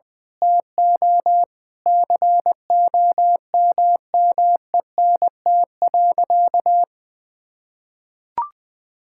Morse code, heard as a keyed tone, about 20 wpm, 700 Hz. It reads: TO COMMENT.
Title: Sentences with 8-Word Phrases - 20wpm